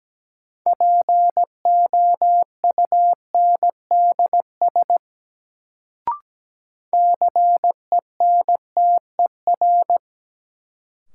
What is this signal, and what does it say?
Morse code, heard as a keyed tone, about 17 wpm, 700 Hz. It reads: POUNDS CENTER
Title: Sentences from Top 1000 Words - Review Part 2 - New Word Drill - 17wpm